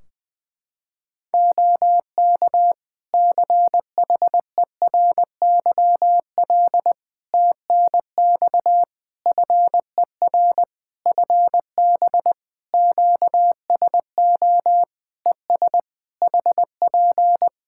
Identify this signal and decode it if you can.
Morse code, heard as a keyed tone, about 20 wpm, 700 Hz. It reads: OK CHERYL TNX FER FB QSO ES HP
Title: Practice QSO - Exchange 4 - Ending - 20wpm